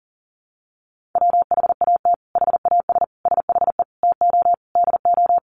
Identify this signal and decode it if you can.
Morse code, heard as a keyed tone, about 37 wpm, 700 Hz. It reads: WHAT HAS SHE TO DO
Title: Sentences from Top 100 Words 40wpm